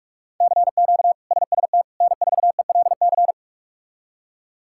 Morse code, heard as a keyed tone, about 46 wpm, 700 Hz. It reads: CQ SST N4ELC
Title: K1USN SST QSO Practice - 45wpm